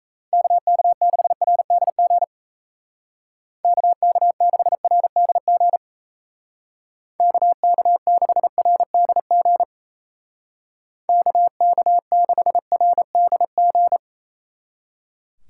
Morse code, heard as a keyed tone, about 35 wpm, 700 Hz. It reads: KK6RDG KK6RDG KK6RDG KK6RDG
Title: Speed Racing - USA 2x3 Call Signs 28wpm